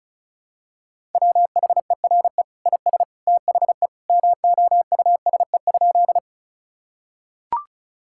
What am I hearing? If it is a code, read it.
Morse code, heard as a keyed tone, about 36 wpm, 700 Hz. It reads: WHERE IS THE MOUSE?
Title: Sentences from Top 1000 Words - Review Part 5 - 35wpm